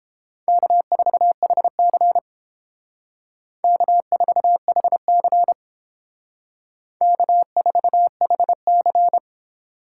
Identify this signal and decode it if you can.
Morse code, heard as a keyed tone, about 28 wpm, 700 Hz. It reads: K4HC K4HC K4HC